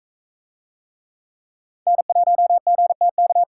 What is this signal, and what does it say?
Morse code, heard as a keyed tone, about 42 wpm, 700 Hz. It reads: N1GTK